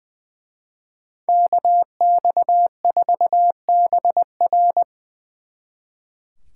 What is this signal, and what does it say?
Morse code, heard as a keyed tone, about 24 wpm, 700 Hz. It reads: KX4BR
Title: Speed Racing - USA Call Signs 20wpm